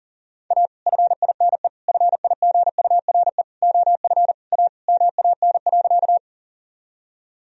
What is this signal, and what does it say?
Morse code, heard as a keyed tone, about 40 wpm, 700 Hz. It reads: A FINE FIGURE OF A MAN.
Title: Sentences from Top 1000 Words - Review Part 2 - 40wpm